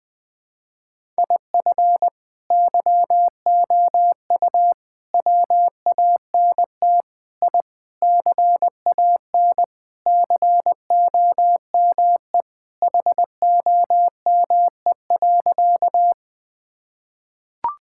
Morse code, heard as a keyed tone, about 21 wpm, 700 Hz. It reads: IF YOU WANT I CAN COME HOME.
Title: Sentences with 3-Word Phrases - 20wpm